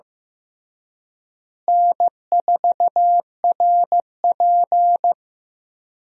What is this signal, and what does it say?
Morse code, heard as a keyed tone, about 16 wpm, 700 Hz. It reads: N4RP
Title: USA Call Signs - 4 Characters - Repeated 3x - 15wpm